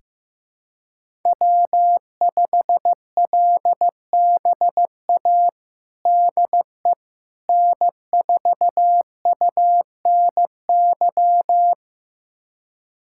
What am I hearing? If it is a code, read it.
Morse code, heard as a keyed tone, about 15 wpm, 700 Hz. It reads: W5LBA DE N4UNY
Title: USA Callsign DE Callsign - 5 Characters - 15wpm